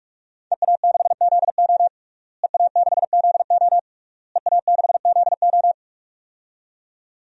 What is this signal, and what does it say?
Morse code, heard as a keyed tone, about 46 wpm, 700 Hz. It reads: EA6ZO EA6ZO EA6ZO
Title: DXCC-100 Call Signs - 5 Characters - Repeated 3x - 45wpm